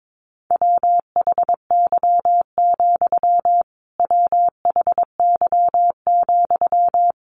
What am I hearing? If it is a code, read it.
Morse code, heard as a keyed tone, about 22 wpm, 700 Hz. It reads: WHY, WHY,
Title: A Sound Of Thunder 22wpm